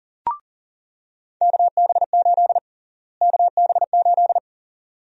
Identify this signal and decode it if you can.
Morse code, heard as a keyed tone, about 40 wpm, 700 Hz. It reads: KB8 KB8
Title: USA Call Sign Prefixes - Repeated 2x - 40wpm